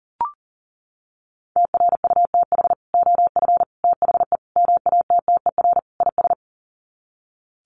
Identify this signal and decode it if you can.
Morse code, heard as a keyed tone, about 40 wpm, 700 Hz. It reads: TRUTH OF THE MATTER IS
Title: Sets of 5 Words Encore2 40wpm